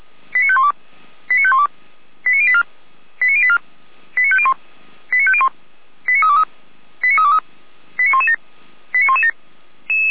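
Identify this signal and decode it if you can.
Five-tone selective call (ZVEI), heard as quick runs of five tones, tones of about 70 ms. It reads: 89512 89512 89084 89084 89501 89501 89323 89323 89117 89117
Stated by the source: ZVEI five-tone selective call 89080 received on SDR.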